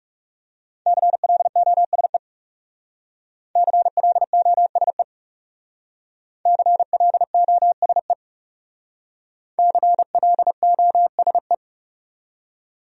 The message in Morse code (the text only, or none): CLOSE CLOSE CLOSE CLOSE